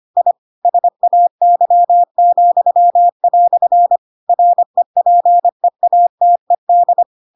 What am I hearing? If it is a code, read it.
Morse code, heard as a keyed tone, about 25 wpm, 700 Hz. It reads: I SAY," REPEATED
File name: Pinocchio / Pinocchio ch7 25 25 ik1hge